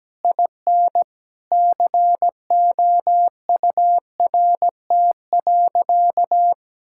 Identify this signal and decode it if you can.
Morse code, heard as a keyed tone, about 17 wpm, 700 Hz. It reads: IN COURT.
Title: Sentences from Top 1600 Words 17wpm